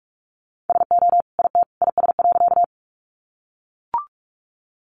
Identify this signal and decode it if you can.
Morse code, heard as a keyed tone, about 44 wpm, 700 Hz. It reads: SO IT IS.
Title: Sentences from Top 100 Words 45wpm